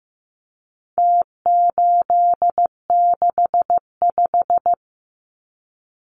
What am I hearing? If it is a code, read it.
Morse code, heard as a keyed tone, about 15 wpm, 700 Hz. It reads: T865
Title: Numbers - Contest Style - 4-Digit 15wpm